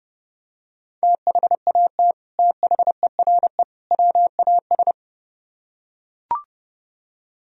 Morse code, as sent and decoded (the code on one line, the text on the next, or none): - .... .- - / - .... .-. / .-- .- ...
THAT THERE WAS